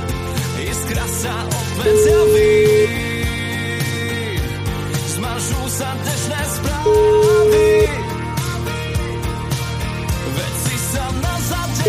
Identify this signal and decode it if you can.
Ringback tone (Europe, 425 Hz), with noise.